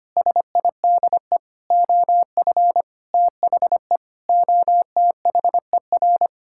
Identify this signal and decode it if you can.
Morse code, heard as a keyed tone, about 25 wpm, 700 Hz. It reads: SIDE OF THE OTHER
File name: practice / 25wpm / Sentences with 3-Word Phrases - 25wpm